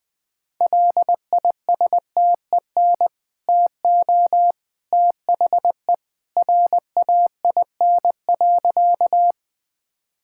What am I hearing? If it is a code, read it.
Morse code, heard as a keyed tone, about 20 wpm, 700 Hz. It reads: LISTEN TO THE RAIN.
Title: Easy Sentences with Hints - 20wpm